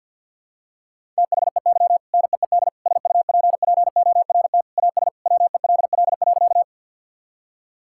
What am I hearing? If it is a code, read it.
Morse code, heard as a keyed tone, about 52 wpm, 700 Hz. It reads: THEY NEED SUPPORT AS WELL.
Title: Sentences from Top 1000 Words - Review Part 5 - 50wpm